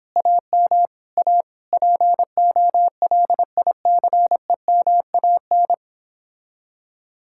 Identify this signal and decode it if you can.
Morse code, heard as a keyed tone, about 29 wpm, 700 Hz. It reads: AM A POLICEMAN